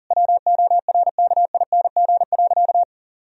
Morse code, heard as a keyed tone, about 37 wpm, 700 Hz. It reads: WORKING.
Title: Sentences from Top 1000 Words - Review Part 4 - 40wpm